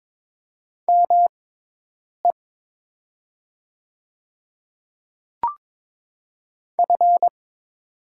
Morse code, heard as a keyed tone, about 22 wpm, spaced as for 8 wpm, 700 Hz. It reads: ME F
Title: 6x Character Spacing - Top 500 Words - 22wpm